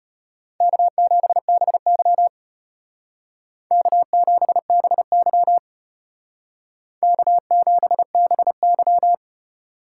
Morse code, heard as a keyed tone, about 32 wpm, 700 Hz. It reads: K7BY K7BY K7BY